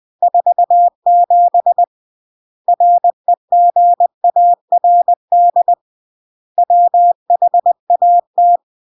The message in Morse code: 47 REGARD WHAT